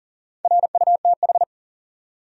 Morse code, heard as a keyed tone, about 43 wpm, 700 Hz. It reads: RUTH
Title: USA Common Names - Max 4 Letters - 40wpm